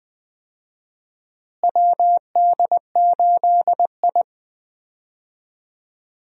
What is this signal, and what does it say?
Morse code, heard as a keyed tone, about 20 wpm, 700 Hz. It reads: WD8I
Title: USA Call Signs - 4 Characters - Repeated 2x - 20wpm